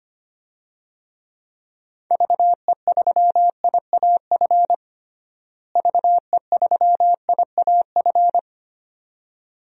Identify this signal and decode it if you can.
Morse code, heard as a keyed tone, about 25 wpm, 700 Hz. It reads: VE3IAF VE3IAF
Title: DXCC-100 Call Signs - 6 Characters - Repeated 2x - 25wpm